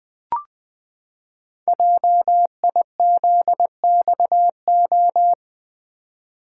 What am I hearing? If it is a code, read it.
Morse code, heard as a keyed tone, about 20 wpm, 700 Hz. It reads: JIZXO
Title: Code Groups - 5 Letters - 20wpm